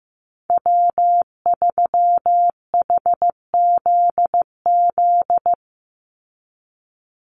Morse code, heard as a keyed tone, about 15 wpm, 700 Hz. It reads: W3HZZ
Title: Call Signs Repeated 3x - 15wpm